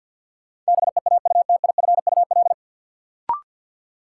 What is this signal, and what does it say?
Morse code, heard as a keyed tone, about 53 wpm, 700 Hz. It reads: BEAUTIFUL
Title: Sentences from Top 900 Words - New Word Drill - 50wpm